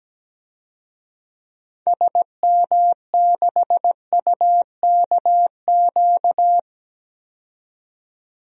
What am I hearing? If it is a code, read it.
Morse code, heard as a keyed tone, about 17 wpm, 700 Hz. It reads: SM6UKQ
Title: DXCC-100 Call Signs - 6 Characters - 17wpm